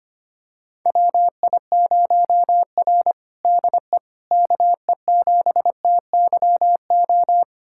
Morse code, heard as a keyed tone, about 25 wpm, 700 Hz. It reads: WI0R DE KE7TYO
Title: USA Callsign DE Callsign 25wpm